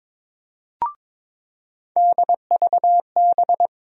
Morse code, heard as a keyed tone, about 22 wpm, 700 Hz. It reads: DVB
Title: USA Call Sign Postfixes - 3 Letters 22wpm